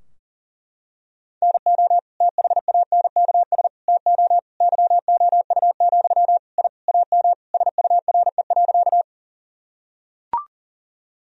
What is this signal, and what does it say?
Morse code, heard as a keyed tone, about 40 wpm, 700 Hz. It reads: NO THANKS TO YOU, I AM SURE.
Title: Sentences from Top 1000 Words - Review Part 3 - 40wpm